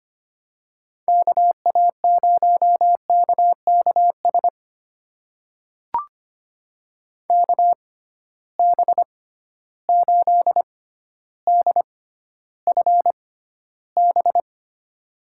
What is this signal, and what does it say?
Morse code, heard as a keyed tone, about 25 wpm, spaced as for 13 wpm, 700 Hz. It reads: KA0KKS KB8DFB
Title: USA Call Signs - 6 Characters - 6x Character Spacing - 25wpm